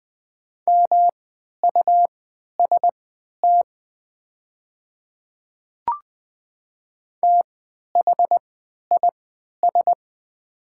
Morse code, heard as a keyed tone, about 20 wpm, spaced as for 11 wpm, 700 Hz. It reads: MUST THIS
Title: 3x Character Spacing - Top 200 Words - 20wpm